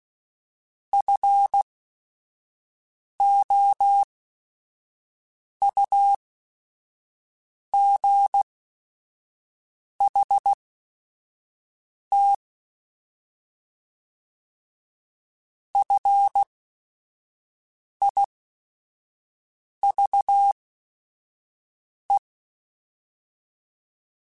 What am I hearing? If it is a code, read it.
Morse code, heard as a keyed tone, about 16 wpm, spaced as for 5 wpm, 800 Hz. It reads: FOUGHT FIVE
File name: F Runs / F Runs mp3s / F Runs @5